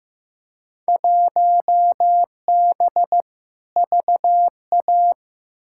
Morse code, heard as a keyed tone, about 15 wpm, 700 Hz. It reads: 1B VA
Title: Field Day Exchanges Encore - 15wpm